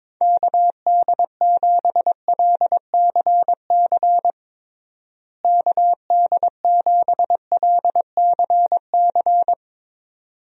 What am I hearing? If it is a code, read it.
Morse code, heard as a keyed tone, about 22 wpm, 700 Hz. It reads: KD7LCC KD7LCC
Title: USA Call Signs - 6 Characters - Repeated 2x - 22wpm